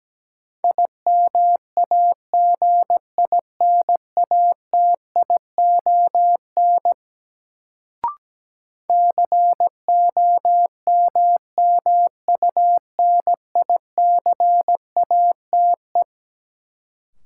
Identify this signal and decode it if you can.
Morse code, heard as a keyed tone, about 17 wpm, 700 Hz. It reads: IMAGINATION COMMUNICATE
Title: Sentences from Top 1300 Words - New Word Drill Alt - 17wpm